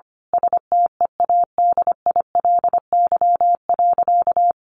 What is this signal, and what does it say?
Morse code, heard as a keyed tone, about 25 wpm, 700 Hz. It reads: STEADILY.
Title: A Sound Of Thunder 25wpm